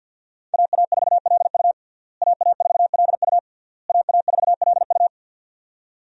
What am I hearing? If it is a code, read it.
Morse code, heard as a keyed tone, about 51 wpm, 700 Hz. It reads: AA4LU AA4LU AA4LU
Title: USA Call Signs - 5 Characters - Repeated 3x - 50wpm